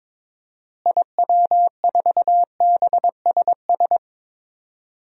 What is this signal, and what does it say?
Morse code, heard as a keyed tone, about 22 wpm, 700 Hz. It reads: IW4BSS
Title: DXCC-100 Call Signs - 6 Characters - Repeated 3x - 22wpm